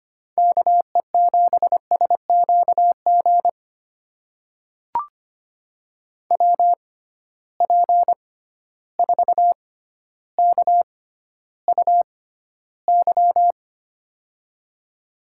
Morse code, heard as a keyed tone, about 25 wpm, spaced as for 13 wpm, 700 Hz. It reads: KE7SQG WP4KUY
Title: USA Call Signs - 6 Characters - 6x Character Spacing - 25wpm